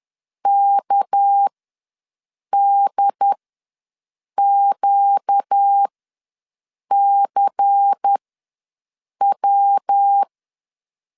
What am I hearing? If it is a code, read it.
Morse code, heard as a keyed tone, about 10 wpm, spaced as for 6 wpm, 800 Hz. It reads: KDQCW
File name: LEZIONI / 6 PKX0 / 6 PK ws 10wpm 3745